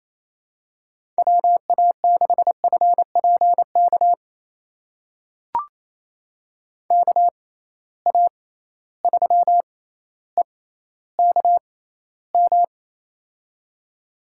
Morse code, heard as a keyed tone, about 28 wpm, spaced as for 14 wpm, 700 Hz. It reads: WA6FPK KA3EKM
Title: USA Call Signs - 6 Characters - 6x Character Spacing - 28wpm